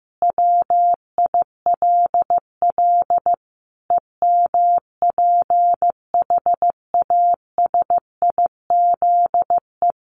Morse code, heard as a keyed tone, about 15 wpm, 700 Hz. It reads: WILL EMPHASIZE
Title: Sentences from Top 1500 Words 15wpm